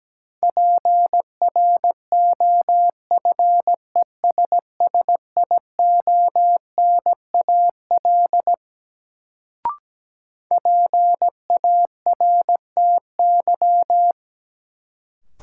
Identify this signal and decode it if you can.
Morse code, heard as a keyed tone, about 17 wpm, 700 Hz. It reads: PROFESSIONAL PARTY
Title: Sentences from Top 1400 Words - New Word Drill Alt - 17wpm